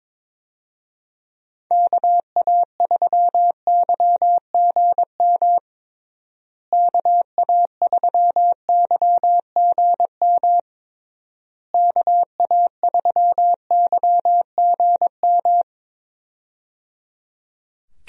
Morse code, heard as a keyed tone, about 22 wpm, 700 Hz. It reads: KA3YGM KA3YGM KA3YGM
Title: USA Call Signs - 6 Characters - Repeated 3x - 22wpm